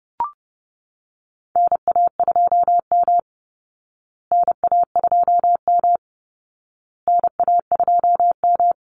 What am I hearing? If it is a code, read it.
Morse code, heard as a keyed tone, about 30 wpm, 700 Hz. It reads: NA2M NA2M NA2M
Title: Call Signs Repeated 2x - 30wpm